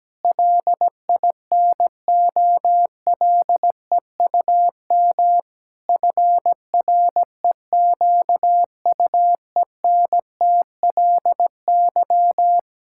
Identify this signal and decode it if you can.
Morse code, heard as a keyed tone, about 17 wpm, 700 Hz. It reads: LINOLEUM FREQUENTLY
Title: Pangram Challenge v2 - 17wpm